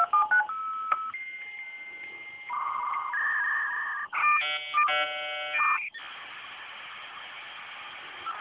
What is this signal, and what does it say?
Dial-up modem handshake.